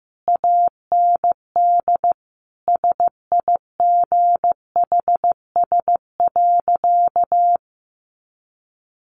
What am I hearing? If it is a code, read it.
Morse code, heard as a keyed tone, about 14 wpm, 700 Hz. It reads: AND SIGHS.
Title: A Sound Of Thunder 15wpm